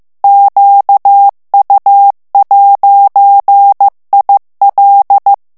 Morse code, heard as a keyed tone, about 15 wpm, 800 Hz. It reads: QU'IL
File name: FRANCAIS TEXTE CLAIR 2 VITESSE 14 MOTS MINUTES